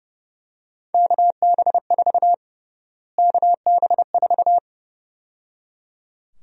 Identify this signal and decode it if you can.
Morse code, heard as a keyed tone, about 30 wpm, 700 Hz. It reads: KB4 KB4
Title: USA Call Sign Prefixes - Repeated 2x - 30wpm